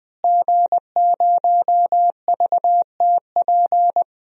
Morse code, heard as a keyed tone, about 20 wpm, 700 Hz. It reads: G0VTP
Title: DXCC-100 Call Signs - 5 Characters - Repeated 3x - 20wpm